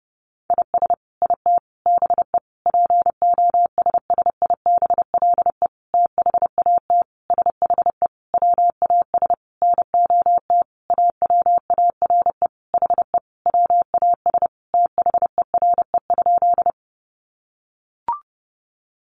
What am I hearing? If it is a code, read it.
Morse code, heard as a keyed tone, about 30 wpm, 700 Hz. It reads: IS IT BE POSSIBLE THAT SHE WAS NOT AWARE HE WAS THERE?
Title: Sentences from Top 1200 Words 30wpm